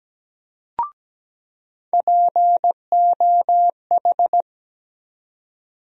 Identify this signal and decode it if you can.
Morse code, heard as a keyed tone, about 17 wpm, 700 Hz. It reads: POH
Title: USA Call Sign Postfixes - 3 Letters 17wpm